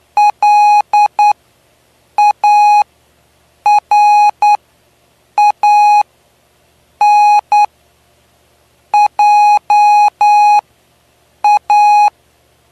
Morse code, heard as a keyed tone, about 9 wpm, spaced as for 6 wpm, 830 Hz. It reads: LARANJA
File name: cw / py2lcd / 09-PY2LCD-Li��o 9